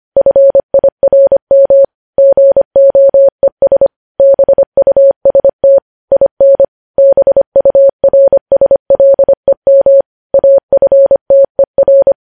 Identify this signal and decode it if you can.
Morse code, heard as a keyed tone, about 25 wpm, 550 Hz. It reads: FIRM GOES BUST IN BURSLEM AFTER
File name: podcast / News headlines-1x-25x25